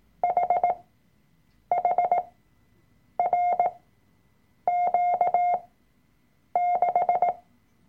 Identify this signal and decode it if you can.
Morse code, heard as a keyed tone, about 18 wpm, spaced as for 7 wpm, 710 Hz. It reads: HHRQ6
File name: Workshop Dateien / CW-Online-Kurs / Lektion 003